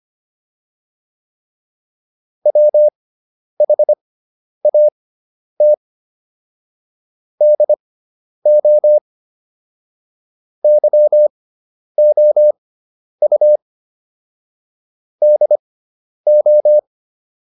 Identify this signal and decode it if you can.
Morse code, heard as a keyed tone, about 25 wpm, spaced as for 10 wpm, 600 Hz. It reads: WHAT DO YOU DO